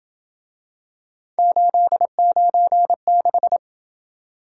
Morse code, heard as a keyed tone, about 27 wpm, 700 Hz. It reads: 896